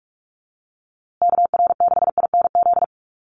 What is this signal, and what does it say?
Morse code, heard as a keyed tone, about 44 wpm, 700 Hz. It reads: KR6INZ
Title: USA Call Signs Repeated 2x 45wpm